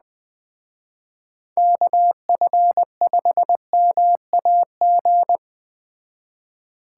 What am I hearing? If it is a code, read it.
Morse code, heard as a keyed tone, about 21 wpm, 700 Hz. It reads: KF5MAG
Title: Speed Racing - USA Call Signs 17wpm